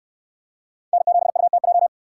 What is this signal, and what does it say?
Morse code, heard as a keyed tone, about 66 wpm, 700 Hz. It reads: N7UTJ